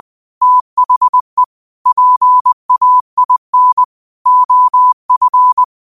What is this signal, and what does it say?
Morse code, heard as a keyed tone, about 20 wpm, 1000 Hz. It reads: THE PAIN OF
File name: qotdfiles / 20wpm